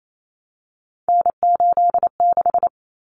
Morse code, heard as a keyed tone, about 28 wpm, 700 Hz. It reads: N86
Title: Common Cut Numbers - 3-Digit 28wpm